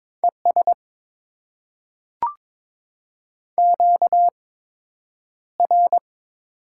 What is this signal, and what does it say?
Morse code, heard as a keyed tone, about 22 wpm, spaced as for 9 wpm, 700 Hz. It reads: ES QR
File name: Top 90 QSO Elements - 8x Character Spacing - 22wpm